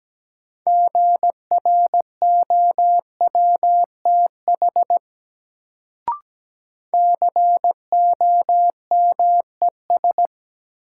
Morse code, heard as a keyed tone, about 17 wpm, 700 Hz. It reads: GROWTH COMES
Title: Sentences from Top 1000 Words - Review Part 5 - New Word Drill - 17wpm